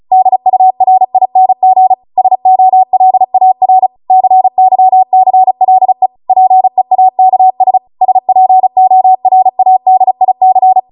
Morse code, heard as a keyed tone, about 36 wpm, 750 Hz. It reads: DURING SOLAR CYCLE PEAKS SPORADIC